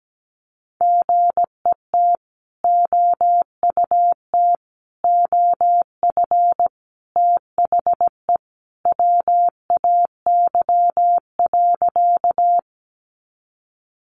Morse code, heard as a keyed tone, about 17 wpm, 700 Hz. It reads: GET OUT OF THE WAY.
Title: Sentences from Top 200 Words 17wpm